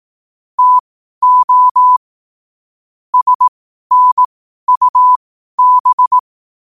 Morse code, heard as a keyed tone, about 18 wpm, spaced as for 12 wpm, 1000 Hz. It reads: TO SNUB
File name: qotdfiles / 12wpm